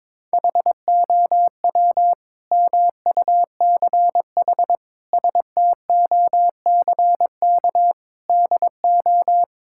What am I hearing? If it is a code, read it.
Morse code, heard as a keyed tone, about 22 wpm, 700 Hz. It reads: HOW MUCH STOCK DO
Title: Sentences from Top 1000 Words - Review Part 5 - 22wpm